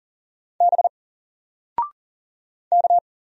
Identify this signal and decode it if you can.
Morse code, heard as a keyed tone, about 40 wpm, 700 Hz. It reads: B K
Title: Single Letter-Number 40wpm